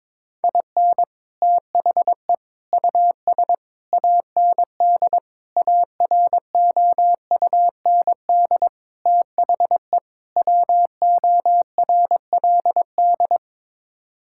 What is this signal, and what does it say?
Morse code, heard as a keyed tone, about 22 wpm, 700 Hz. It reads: IN THE US AND AROUND THE WORLD
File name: practice / 22wpm / Sets of 7 Words - 22wpm